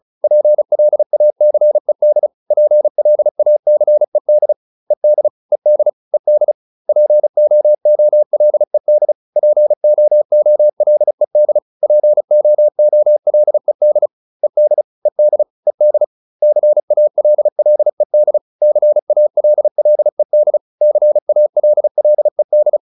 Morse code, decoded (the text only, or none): PLACED PLACED ED ED ED POOLED POOLED POOLED ED ED ED CALLED CALLED CALLED